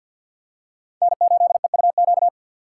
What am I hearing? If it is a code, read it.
Morse code, heard as a keyed tone, about 52 wpm, 700 Hz. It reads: N8EUQ
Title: USA Call Signs - 5 Characters - 6x Character Spacing - 50wpm